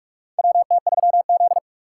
Morse code, heard as a keyed tone, about 47 wpm, 700 Hz. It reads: WT3Z
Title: USA Call Signs - 4 Characters - 8x Character Spacing - 45wpm